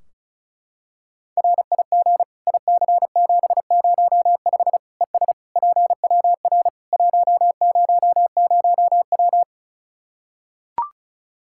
Morse code, heard as a keyed tone, about 36 wpm, 700 Hz. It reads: RIG IC705 ES PWR 100W
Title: Practice QSO - Exchange 2 - Info - 35wpm